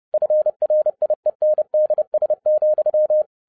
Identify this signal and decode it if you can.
Morse code, heard as a keyed tone, about 30 wpm, 610 Hz. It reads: FRIENDS,
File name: 2011-07-mp3 / n40 EXCITING QRP CW OPERATION 30wpm